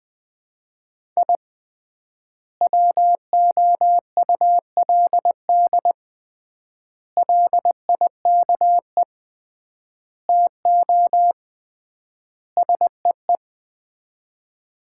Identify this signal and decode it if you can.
Morse code, heard as a keyed tone, about 20 wpm, 700 Hz. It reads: I WOULD LIKE TO SEE
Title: 3x Word-Spacing - Sets of 5 Words - 20wpm